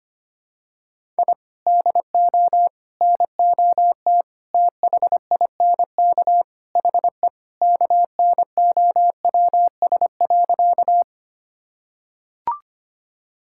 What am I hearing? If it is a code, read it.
Morse code, heard as a keyed tone, about 25 wpm, 700 Hz. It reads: I DO NOT THINK HE KNOWS.
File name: practice / 25wpm / Sentences with 4-Word Phrases - 25wpm